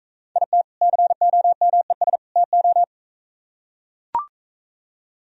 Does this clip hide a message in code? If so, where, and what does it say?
Yes: Morse code from 0.4 to 2.8 s, heard as a keyed tone, about 44 wpm, 700 Hz. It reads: IT COMES TO